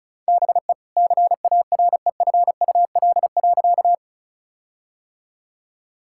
Morse code, heard as a keyed tone, about 34 wpm, 700 Hz. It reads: BE CAREFUL.
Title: Sentences from Top 1600 Words 35wpm